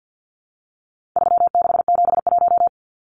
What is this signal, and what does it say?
Morse code, heard as a keyed tone, about 50 wpm, 700 Hz. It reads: 3671
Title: Numbers - 4-Digit 50wpm